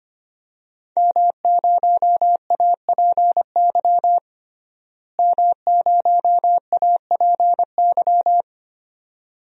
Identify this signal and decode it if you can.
Morse code, heard as a keyed tone, about 25 wpm, 700 Hz. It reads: M0APY M0APY
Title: DXCC-100 Call Signs - 5 Characters - Repeated 2x - 25wpm